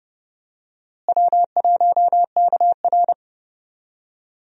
Morse code, heard as a keyed tone, about 30 wpm, 700 Hz. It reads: W1KR
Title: USA Call Signs - 4 Characters - Repeated 3x - 30wpm